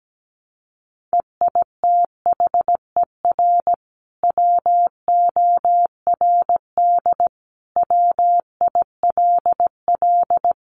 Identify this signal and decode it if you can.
Morse code, heard as a keyed tone, about 17 wpm, 700 Hz. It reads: EITHER WORD WILL
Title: Sentences from Top 600 Words 17wpm